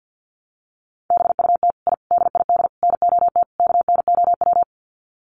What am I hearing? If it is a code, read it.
Morse code, heard as a keyed tone, about 50 wpm, 700 Hz. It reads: BUT I DID NOT KNOW
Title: Sets of 5 Words Encore 50wpm